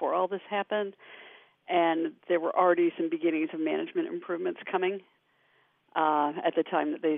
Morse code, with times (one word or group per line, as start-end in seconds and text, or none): none